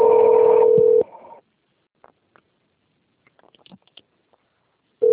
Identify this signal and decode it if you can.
Ringback tone (North America), 440 + 480 Hz.
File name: out- - - - - .17861